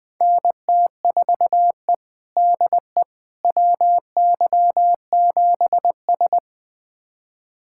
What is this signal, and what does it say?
Morse code, heard as a keyed tone, about 20 wpm, 700 Hz. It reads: NT4E DE WY7S
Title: USA Callsign DE Callsign - 4 Characters - 20wpm